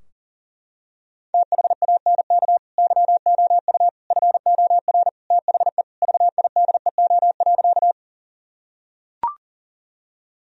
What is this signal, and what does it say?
Morse code, heard as a keyed tone, about 37 wpm, 700 Hz. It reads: THANK YOU FOR THE VIDEO.
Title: Sentences from Top 1000 Words - Review Part 4 - 40wpm